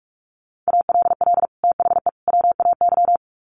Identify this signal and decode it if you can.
Morse code, heard as a keyed tone, about 44 wpm, 700 Hz. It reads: ALL THE WAY
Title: Sets of 3 Words 45wpm